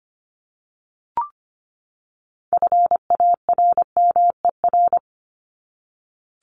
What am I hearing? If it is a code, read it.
Morse code, heard as a keyed tone, about 25 wpm, 700 Hz. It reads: FARMER
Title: Top 1751-2000 Words 25wpm